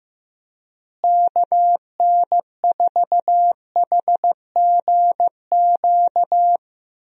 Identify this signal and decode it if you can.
Morse code, heard as a keyed tone, about 15 wpm, 700 Hz. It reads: KN4HGQ
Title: USA Call Signs - 6 Characters - Repeated 3x - 15wpm